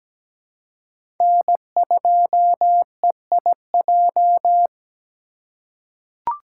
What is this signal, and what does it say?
Morse code, heard as a keyed tone, about 17 wpm, 700 Hz. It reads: N2EIJ
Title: USA Call Signs - 5 Characters - 3x Character Spacing - 17wpm